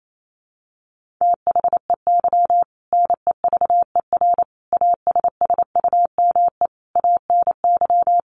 Morse code, heard as a keyed tone, about 28 wpm, 700 Hz. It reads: THEY NEVER ASSUME ANY
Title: Sentences from Top 1500 Words 28wpm